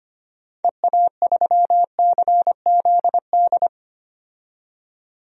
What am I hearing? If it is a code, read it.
Morse code, heard as a keyed tone, about 25 wpm, 700 Hz. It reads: EA3CZD
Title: DXCC-100 Call Signs - 6 Characters - 25wpm